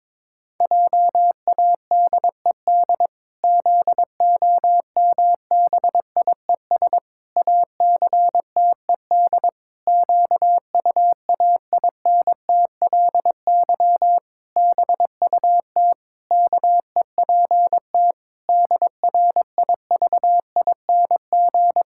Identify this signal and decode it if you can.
Morse code, heard as a keyed tone, about 22 wpm, 700 Hz. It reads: JADED ZOMBIES ACTED QUAINTLY BUT KEPT DRIVING
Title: Pangram Challenge v2 - 22wpm